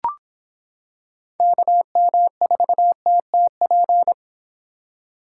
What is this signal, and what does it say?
Morse code, heard as a keyed tone, about 25 wpm, 700 Hz. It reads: KM4TTP